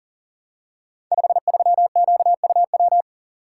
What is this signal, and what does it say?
Morse code, heard as a keyed tone, about 42 wpm, 690 Hz. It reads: 53QUW
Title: Code Groups - 5 Letter:Numbers - 40wpm